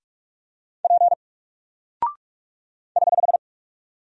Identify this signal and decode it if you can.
Morse code, heard as a keyed tone, about 45 wpm, 700 Hz. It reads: P <HH>
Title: Single Character (Letter:Number:Punctuation:Prosign) 45wpm